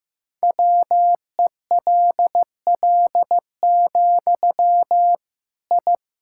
Morse code, heard as a keyed tone, about 15 wpm, 700 Hz. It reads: WELL, I
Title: Sentences from Top 1000 Words - Review Part 5 - 15wpm